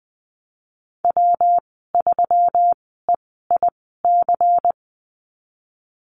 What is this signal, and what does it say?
Morse code, heard as a keyed tone, about 20 wpm, spaced as for 15 wpm, 700 Hz. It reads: W3EIC
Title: 179 Lesson 29g - Slash - Call Signs 20wpm